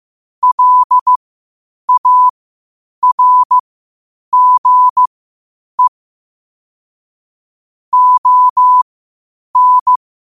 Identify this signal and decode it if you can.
Morse code, heard as a keyed tone, about 15 wpm, spaced as for 8 wpm, 1000 Hz. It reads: LARGE ON
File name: qotdfiles / 8wpm